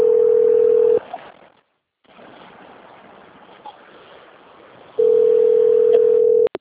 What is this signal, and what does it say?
Ringback tone (North America), 440 + 480 Hz.
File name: out- - - - - .57030